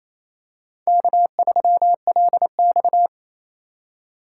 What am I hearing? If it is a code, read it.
Morse code, heard as a keyed tone, about 28 wpm, 700 Hz. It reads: K3LX
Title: USA Call Signs - 4 Characters - 4x Character Spacing - 28wpm